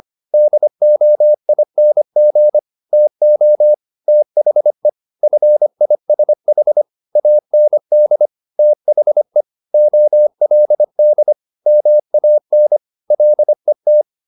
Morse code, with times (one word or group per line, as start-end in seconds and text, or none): 0.3-2.6 DOING
2.9-3.7 TO
4.1-4.9 THE
5.2-6.8 FISH
7.2-8.3 AND
8.6-9.4 THE
9.7-11.3 OLD
11.7-12.8 MAN
13.1-14.0 LET